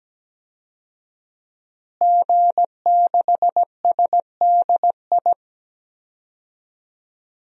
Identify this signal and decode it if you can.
Morse code, heard as a keyed tone, about 17 wpm, 700 Hz. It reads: G6SDI
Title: DXCC-100 Call Signs - 5 Characters - Repeated 2x - 17wpm